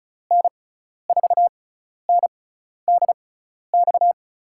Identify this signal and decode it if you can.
Morse code, heard as a keyed tone, about 36 wpm, spaced as for 12 wpm, 700 Hz. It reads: N4NDX
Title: USA Call Signs - 6 Characters - 6x Character Spacing - 35wpm